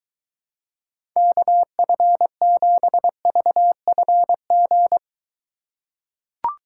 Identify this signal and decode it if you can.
Morse code, heard as a keyed tone, about 23 wpm, 700 Hz. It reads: KF7VFG